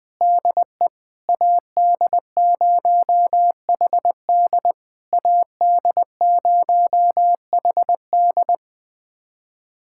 Morse code, heard as a keyed tone, about 20 wpm, 690 Hz. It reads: DE AD0HD AD0HD
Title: SOTA QSO Practice 20wpm